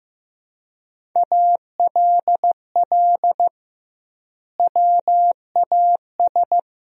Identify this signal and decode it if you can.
Morse code, heard as a keyed tone, about 15 wpm, 700 Hz. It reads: ALL WAS
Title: 2x Word-Spacing - Sentences from Top 500 Words - 15wpm